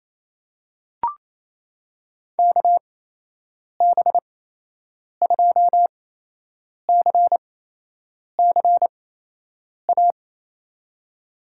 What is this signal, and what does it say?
Morse code, heard as a keyed tone, about 28 wpm, spaced as for 8 wpm, 700 Hz. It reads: KB2CCA
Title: USA Call Signs - 6 Characters - 8x Character Spacing - 28wpm